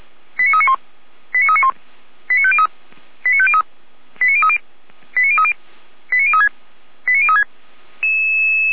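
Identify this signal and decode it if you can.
Five-tone selective call (ZVEI), heard as quick runs of five tones, tones of about 70 ms. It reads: 89391 89391 89503 89503 89030 89030 89036 89036
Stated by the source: ZVEI five-tone selective call 89050 received on SDR.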